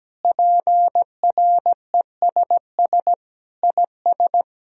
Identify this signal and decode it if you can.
Morse code, heard as a keyed tone, about 17 wpm, 700 Hz. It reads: PRESS IS
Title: Sentences from Top 1000 Words - Review Part 1 - 17wpm